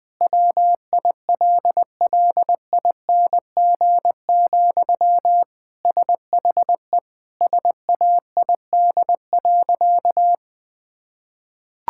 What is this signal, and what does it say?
Morse code, heard as a keyed tone, about 20 wpm, 700 Hz. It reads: WILLING, SHE SAID.
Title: Sentences from Top 1600 Words 20wpm